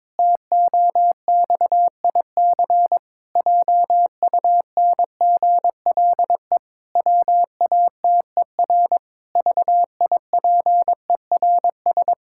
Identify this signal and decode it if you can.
Morse code, heard as a keyed tone, about 22 wpm, 700 Hz. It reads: TOXIC JUNGLE WATER VIPERS
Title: Pangram Challenge v2 - 22wpm